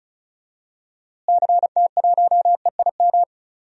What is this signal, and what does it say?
Morse code, heard as a keyed tone, about 36 wpm, 700 Hz. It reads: CT1EIM
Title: DXCC-100 Call Signs - 6 Characters - Repeated 3x - 35wpm